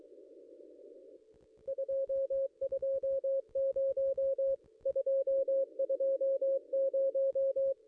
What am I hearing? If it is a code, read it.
Morse code, heard as a keyed tone, about 24 wpm, 540 Hz. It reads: 220 220
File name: contests / ss-cw-k5tr-op-n5rz -2200